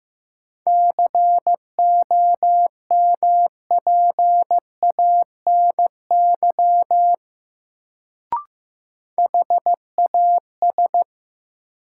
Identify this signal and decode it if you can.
Morse code, heard as a keyed tone, about 16 wpm, 700 Hz. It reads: COMPANY HAS
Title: Sets of 2 Words - New Word Drill - 15wpm